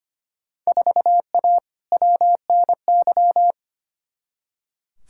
Morse code, heard as a keyed tone, about 25 wpm, 700 Hz. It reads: 4A WNY
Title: Field Day Exchanges Encore - 25wpm